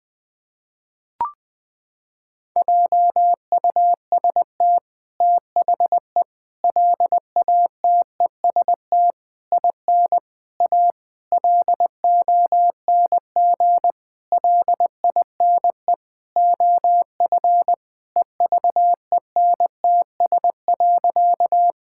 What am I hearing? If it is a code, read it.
Morse code, heard as a keyed tone, about 20 wpm, 700 Hz. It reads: JUST THE LATEST IN A LONG LINE OF EVENTS.
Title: Sentences with 8-Word Phrases - 20wpm